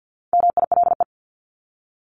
Morse code, heard as a keyed tone, about 50 wpm, 700 Hz. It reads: MILE
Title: ES:S Suffix - Top 100 Words with Base Words 50wpm